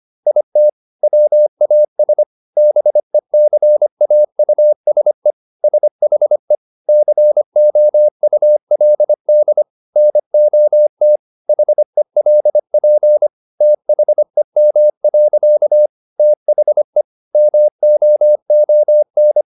Morse code, heard as a keyed tone, about 25 wpm, 600 Hz. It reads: IT WAS BECAUSE SHE COULD NOT HELP THEM. THE MOON
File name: the old man and the sea part 1 25wpm